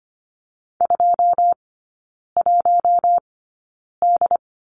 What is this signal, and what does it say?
Morse code, heard as a keyed tone, about 25 wpm, spaced as for 9 wpm, 700 Hz. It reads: 21D
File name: Lesson 17e - Letter u - Three Characters 25wpm